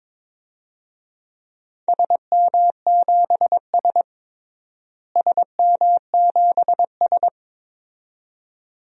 Morse code, heard as a keyed tone, about 22 wpm, 700 Hz. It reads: SM7S SM7S
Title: DXCC-100 Call Signs - 4 Characters - Repeated 2x - 22wpm